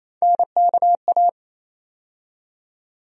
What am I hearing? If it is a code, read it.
Morse code, heard as a keyed tone, about 28 wpm, 700 Hz. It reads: NKA
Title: USA Call Sign Postfixes - 3 Letters 28wpm